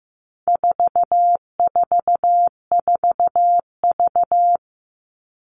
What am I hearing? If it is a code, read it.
Morse code, heard as a keyed tone, about 15 wpm, 700 Hz. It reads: 444V
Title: All Cut Numbers - 4-Digit 15wpm